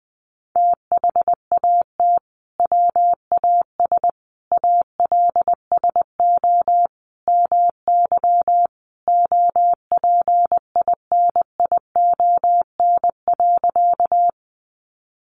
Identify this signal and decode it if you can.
Morse code, heard as a keyed tone, about 20 wpm, 700 Hz. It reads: THAT WAS ALSO MY OPINION.
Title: Sentences from Top 1200 Words 20wpm